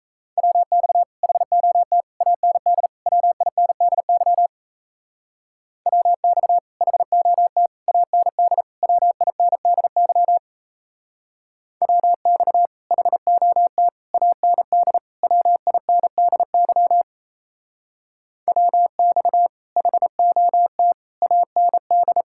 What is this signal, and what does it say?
Morse code, heard as a keyed tone, about 33 wpm, 700 Hz. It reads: WX HOT AND WINDY WX HOT AND WINDY WX HOT AND WINDY WX HOT AND
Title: Speed-Racing Common QSO Phrases Encore - 28wpm